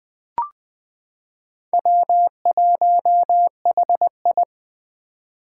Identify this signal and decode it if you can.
Morse code, heard as a keyed tone, about 21 wpm, 700 Hz. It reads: W1HI
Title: Four Challenging Letter-Number Combinations 20wpm